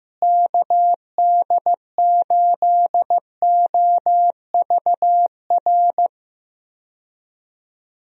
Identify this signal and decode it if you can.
Morse code, heard as a keyed tone, about 15 wpm, 700 Hz. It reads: KD8OVR
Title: USA Call Signs - 6 Characters - Repeated 2x - 15wpm